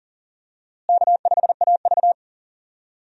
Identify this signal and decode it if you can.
Morse code, heard as a keyed tone, about 40 wpm, 700 Hz. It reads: K5AV
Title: USA Call Signs - 4 Characters - Repeated 3x - 40wpm